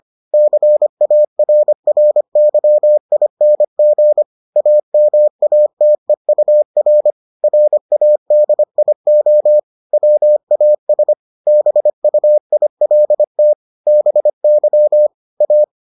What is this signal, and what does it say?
Morse code, heard as a keyed tone, about 25 wpm, 600 Hz. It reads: CARRYING AMATEUR RADIO WAS BUILT BY A